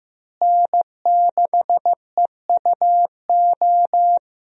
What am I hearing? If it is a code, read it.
Morse code, heard as a keyed tone, about 15 wpm, 700 Hz. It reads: N6EUO
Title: USA Call Signs - 5 Characters - Repeated 3x - 15wpm